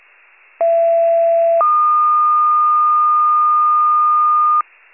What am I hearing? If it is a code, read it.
Two-tone page, heard as a one-second tone, then a three-second tone, about 670 then 1220 Hz. It is heard from 0.6 to 4.6 s.
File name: pager / 39 RESCUE / wcalert